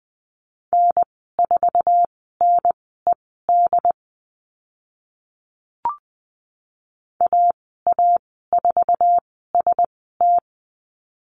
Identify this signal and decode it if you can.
Morse code, heard as a keyed tone, about 20 wpm, spaced as for 14 wpm, 700 Hz. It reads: N4NED AA4ST
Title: 069 Lesson 12g - Letter d - Call Signs 20wpm